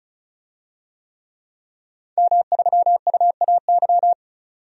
Morse code, heard as a keyed tone, about 34 wpm, 700 Hz. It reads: M3UAY
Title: DXCC-100 Call Signs - 5 Characters - 35wpm